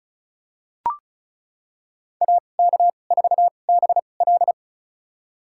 Morse code, heard as a keyed tone, about 35 wpm, spaced as for 25 wpm, 690 Hz. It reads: AK4BL
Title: USA Call Signs - 5 Characters - 2x Character Spacing - 35wpm